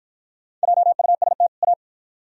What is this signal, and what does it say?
Morse code, heard as a keyed tone, about 50 wpm, 700 Hz. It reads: JUST A